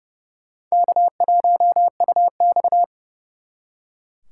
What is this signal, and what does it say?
Morse code, heard as a keyed tone, about 39 wpm, 700 Hz. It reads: K1UX